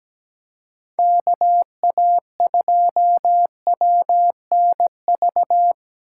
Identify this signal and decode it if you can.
Morse code, heard as a keyed tone, about 17 wpm, 700 Hz. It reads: KA2WNV
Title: USA Call Signs - 2x Character Spacing - 17wpm